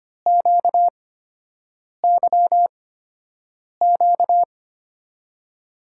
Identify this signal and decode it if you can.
Morse code, heard as a keyed tone, about 25 wpm, spaced as for 7 wpm, 700 Hz. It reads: QYQ
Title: USA Call Signs - 6 Characters - 8x Character Spacing - 25wpm